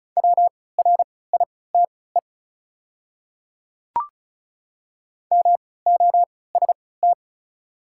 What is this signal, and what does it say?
Morse code, heard as a keyed tone, about 36 wpm, spaced as for 20 wpm, 690 Hz. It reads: WRITE MOST
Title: 3x Character Spacing - Top 100 Words - 35wpm